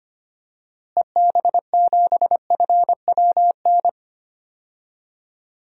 Morse code, heard as a keyed tone, about 25 wpm, 700 Hz. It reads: EB7FWN
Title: DXCC-100 Call Signs - 6 Characters - Repeated 2x - 25wpm